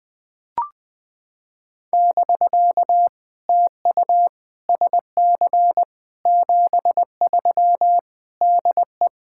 Morse code, heard as a keyed tone, about 19 wpm, 700 Hz. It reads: <BK> TU SC 73 DE
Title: POTA QSO Practice - 20wpm